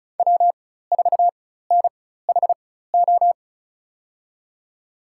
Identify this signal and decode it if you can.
Morse code, heard as a keyed tone, about 36 wpm, spaced as for 16 wpm, 700 Hz. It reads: W4NHO
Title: USA Call Signs - 5 Characters - 4x Character Spacing - 35wpm